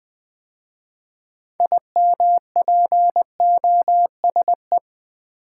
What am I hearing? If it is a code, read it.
Morse code, heard as a keyed tone, about 20 wpm, 700 Hz. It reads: IMPOSE
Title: 6 Letter Words - First 3 Letters Given - 20wpm